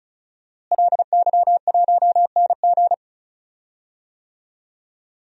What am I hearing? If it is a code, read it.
Morse code, heard as a keyed tone, about 36 wpm, 700 Hz. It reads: LY1NG
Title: DXCC-100 Call Signs - 5 Characters - Repeated 3x - 35wpm